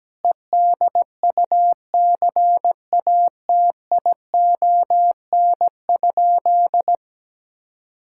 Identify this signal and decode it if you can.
Morse code, heard as a keyed tone, about 17 wpm, 700 Hz. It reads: EDUCATION?
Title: Sentences from Top 1000 Words - Review Part 6 - 17wpm